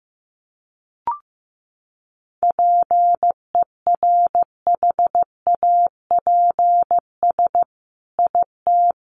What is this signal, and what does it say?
Morse code, heard as a keyed tone, about 15 wpm, 700 Hz. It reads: PERHAPS IT
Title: Sentences from Top 1500 Words 15wpm